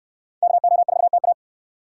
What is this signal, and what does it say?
Morse code, heard as a keyed tone, about 63 wpm, 700 Hz. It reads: KO4TA